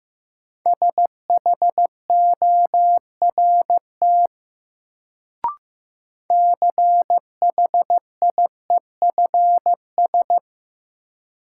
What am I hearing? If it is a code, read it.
Morse code, heard as a keyed tone, about 15 wpm, 700 Hz. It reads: SHORT CHIEFS
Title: Sets of 5 Words Encore - New Word Drill - 15wpm